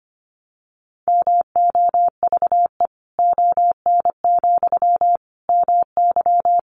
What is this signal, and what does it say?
Morse code, heard as a keyed tone, about 25 wpm, 700 Hz. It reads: MOVE ON, MY
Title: Sentences from Top 500 Words 25wpm